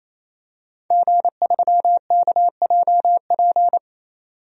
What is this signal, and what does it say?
Morse code, heard as a keyed tone, about 28 wpm, 700 Hz. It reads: G3KJP
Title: DXCC-100 Call Signs - 5 Characters - 28wpm